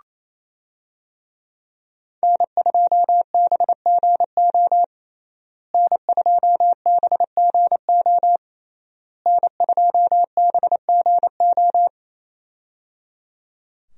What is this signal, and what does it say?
Morse code, heard as a keyed tone, about 27 wpm, 700 Hz. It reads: N2BGO N2BGO N2BGO
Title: USA Call Signs - 5 Characters - Repeated 3x - 28wpm